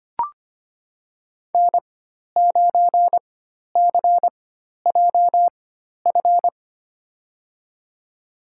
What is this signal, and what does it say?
Morse code, heard as a keyed tone, about 25 wpm, spaced as for 12 wpm, 700 Hz. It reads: N9CJF
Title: USA Call Signs - 5 Characters - 4x Character Spacing - 25wpm